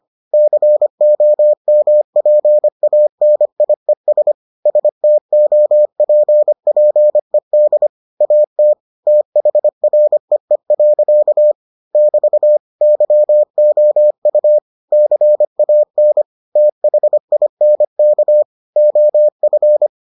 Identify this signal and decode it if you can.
Morse code, heard as a keyed tone, about 25 wpm, 600 Hz. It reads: COMPANIES STOPPED AT THREE. <BT> YOU CAN THINK OF